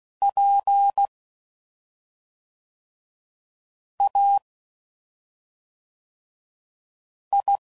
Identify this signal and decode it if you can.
Morse code, heard as a keyed tone, about 16 wpm, 800 Hz. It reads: P A I
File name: Suffixes / Special ING's / SlowToFaster ING's